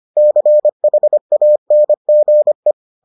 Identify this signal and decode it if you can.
Morse code, heard as a keyed tone, about 25 wpm, 600 Hz. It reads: CHANGE